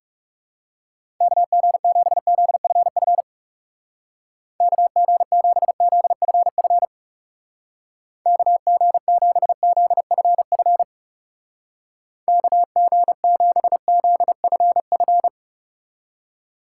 Morse code, heard as a keyed tone, about 37 wpm, 700 Hz. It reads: KG7ZFF KG7ZFF KG7ZFF KG7ZFF
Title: Speed Racing - USA 2x3 Call Signs 30wpm